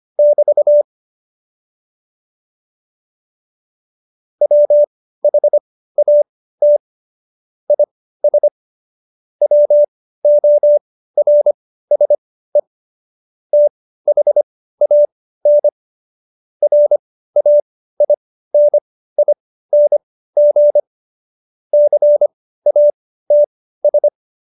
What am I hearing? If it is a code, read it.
Morse code, heard as a keyed tone, about 25 wpm, spaced as for 15 wpm, 590 Hz. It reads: <BT> WHAT IS WORSE THAN RAINING CATS
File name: Jokes-25-char-15-eff